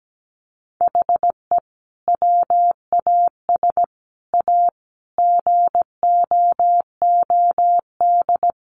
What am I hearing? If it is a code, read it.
Morse code, heard as a keyed tone, about 17 wpm, 700 Hz. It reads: HE WAS A GOOD
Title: Sentences from Top 1300 Words 17wpm